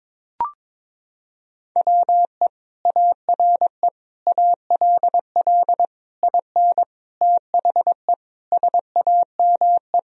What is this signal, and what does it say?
Morse code, heard as a keyed tone, about 21 wpm, 700 Hz. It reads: WE ARE ALL IN THE SAME
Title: Sentences from Top 1000 Words - Review Part 3 - 22wpm